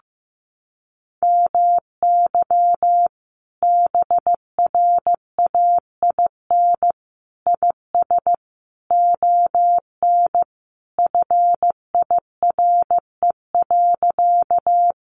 Morse code, heard as a keyed tone, about 15 wpm, 700 Hz. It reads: MY BRAIN IS ON FIRE.
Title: Sentences from Top 1100 Words 15wpm